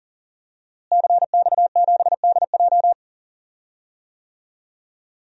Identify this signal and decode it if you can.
Morse code, heard as a keyed tone, about 40 wpm, 700 Hz. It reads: CX7DJ
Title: DXCC-100 Call Signs - 5 Characters - Repeated 3x - 40wpm